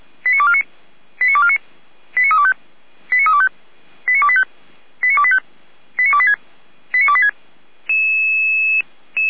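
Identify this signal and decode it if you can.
Five-tone selective call (ZVEI), heard as quick runs of five tones, tones of about 70 ms. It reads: 89249 89249 89325 89325 89285 89285 89286 89286
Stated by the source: ZVEI five-tone selective call 89315 received via SDR, from the audio.f85360.feuerwehr-orlamuende.org fire brigade server.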